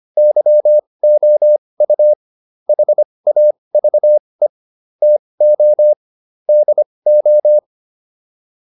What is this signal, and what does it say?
Morse code, heard as a keyed tone, about 25 wpm, spaced as for 20 wpm, 600 Hz. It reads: YOU HAVE TO DO